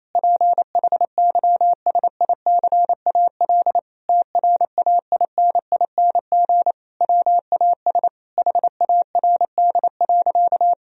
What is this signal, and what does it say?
Morse code, heard as a keyed tone, about 28 wpm, 700 Hz. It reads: PHYSICAL TRAINING WAS HARD.
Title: Sentences from Top 1000 Words - Review Part 4 - 28wpm